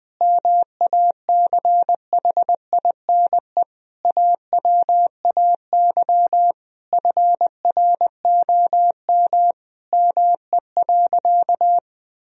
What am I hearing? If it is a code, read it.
Morse code, heard as a keyed tone, about 20 wpm, 700 Hz. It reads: MACHINE AWAY FROM ME.
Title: Sentences from Top 1000 Words - Review Part 2 - 20wpm